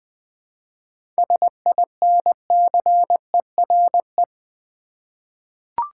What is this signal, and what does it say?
Morse code, heard as a keyed tone, about 20 wpm, 700 Hz. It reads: SINCERE
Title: 7 Letter Words - First 3 Letters Given - 20wpm